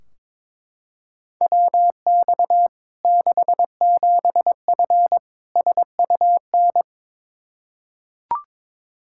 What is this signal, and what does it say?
Morse code, heard as a keyed tone, about 22 wpm, 700 Hz. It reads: WX 67F SUN
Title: K1USN SST QSO Practice 22wpm